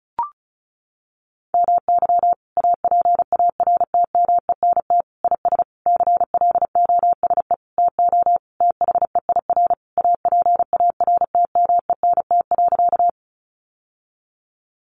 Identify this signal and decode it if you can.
Morse code, heard as a keyed tone, about 34 wpm, 700 Hz. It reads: MY APARTMENT IS CLOSE TO THEIR APARTMENT.
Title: Sentences from Top 900 Words 35wpm